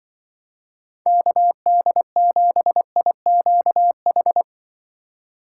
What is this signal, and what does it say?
Morse code, heard as a keyed tone, about 24 wpm, 700 Hz. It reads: KD7IQH
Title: Speed Racing - USA 2x3 Call Signs 20wpm